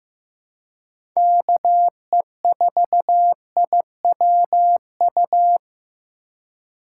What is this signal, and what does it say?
Morse code, heard as a keyed tone, about 15 wpm, 700 Hz. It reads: KE4IWU
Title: USA Call Signs - 3x Character Spacing - 15wpm